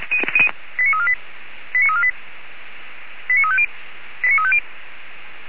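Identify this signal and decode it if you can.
Five-tone selective call (ZVEI), heard as quick runs of five tones, tones of about 70 ms. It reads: 89348 89348 89350
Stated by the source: ZVEI five-tone selective call received by SDR.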